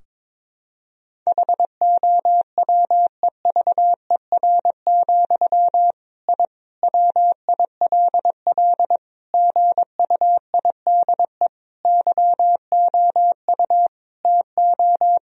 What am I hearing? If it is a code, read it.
Morse code, heard as a keyed tone, about 21 wpm, 700 Hz. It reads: HOWEVER, I WILL GUIDE YOU TO
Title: Sentences from Top 1000 Words - Review Part 3 - 22wpm